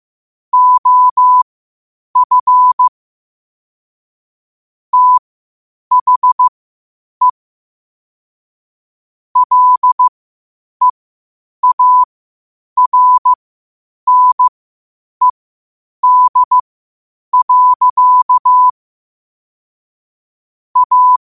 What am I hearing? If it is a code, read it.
Morse code, heard as a keyed tone, about 15 wpm, spaced as for 8 wpm, 1000 Hz. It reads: OF THE LEARNED. A
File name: qotdfiles / 8wpm